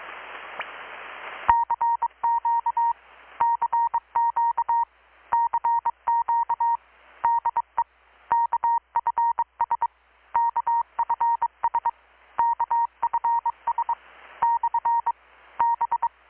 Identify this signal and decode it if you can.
Morse code, heard as a keyed tone, about 22 wpm, 930 Hz. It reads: CQ CQ CQ DE KFS KFS KFS / B
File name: blog / swl / 6-abril-2019-12695-USB-KPS